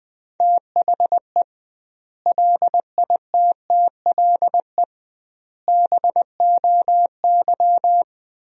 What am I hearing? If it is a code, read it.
Morse code, heard as a keyed tone, about 20 wpm, 700 Hz. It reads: THE LITTLE BOY
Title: 2x Word-Spacing - Sentences from Top 500 Words - 20wpm